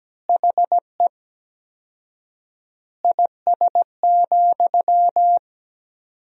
Morse code, heard as a keyed tone, about 17 wpm, 700 Hz. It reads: HE IS,
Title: 4x Word-Spacing - Sentences from Top 100 Words - 17wpm